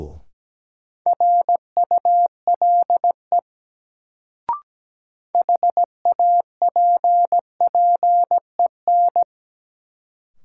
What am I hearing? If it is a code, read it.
Morse code, heard as a keyed tone, about 17 wpm, 700 Hz. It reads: RULE HAPPEN
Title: Sentences from Top 400 Words - New Word Drill - 17wpm